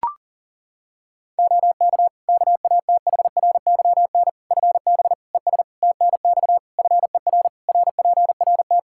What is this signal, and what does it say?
Morse code, heard as a keyed tone, about 37 wpm, 700 Hz. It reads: OK KATHRYN FB ES TNX FER RPRT
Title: K1USN SST QSO Practice 40wpm